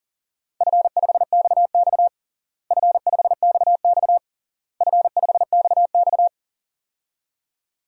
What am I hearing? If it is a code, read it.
Morse code, heard as a keyed tone, about 40 wpm, 700 Hz. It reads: F5XX F5XX F5XX
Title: DXCC-100 Call Signs - 4 Characters - Repeated 3x - 40wpm